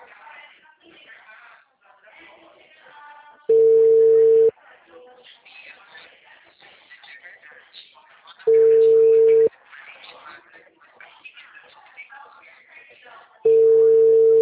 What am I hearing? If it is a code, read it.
Ringback tone (China, 450 Hz), with speech.